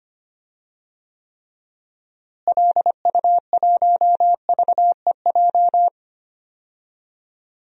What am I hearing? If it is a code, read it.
Morse code, heard as a keyed tone, about 25 wpm, 700 Hz. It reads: LU1VEJ
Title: DXCC-100 Call Signs - 6 Characters - Repeated 3x - 25wpm